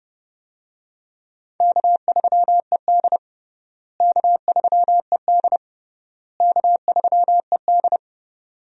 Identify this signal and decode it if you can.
Morse code, heard as a keyed tone, about 30 wpm, 700 Hz. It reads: K3ED K3ED K3ED
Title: USA Call Signs - 4 Characters - Repeated 3x - 30wpm